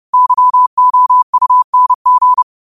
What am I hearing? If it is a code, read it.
Morse code, heard as a keyed tone, about 30 wpm, 1000 Hz. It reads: YOUNG
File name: qotdfiles / 30wpm